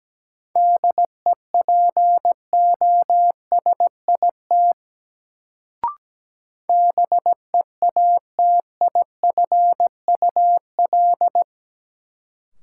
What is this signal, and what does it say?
Morse code, heard as a keyed tone, about 17 wpm, 700 Hz. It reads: DEPOSIT BEATIFUL
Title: Sentences from Top 1300 Words - New Word Drill - 17wpm